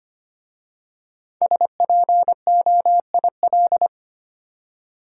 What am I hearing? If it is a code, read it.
Morse code, heard as a keyed tone, about 25 wpm, 700 Hz. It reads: SPOIL
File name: 5 Letter Words - First 3 Letters Given - 25wpm